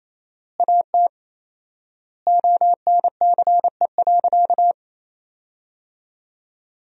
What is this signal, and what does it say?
Morse code, heard as a keyed tone, about 28 wpm, 700 Hz. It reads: AT ONCE.
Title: 4x Word-Spacing - Sentences from Top 400 Words - 28wpm